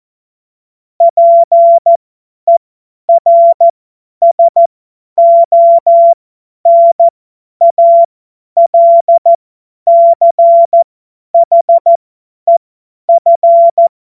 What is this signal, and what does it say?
Morse code, heard as a keyed tone, about 14 wpm, spaced as for 10 wpm, 680 Hz. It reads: PERSONALCHEF
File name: Witz-22-Juli-2025-50-F70